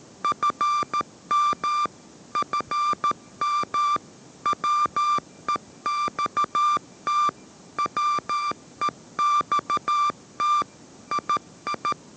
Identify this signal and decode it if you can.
Morse code, heard as a keyed tone, about 13 wpm, 1180 Hz. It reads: FM FM WEXT WEXT II
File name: 14wpm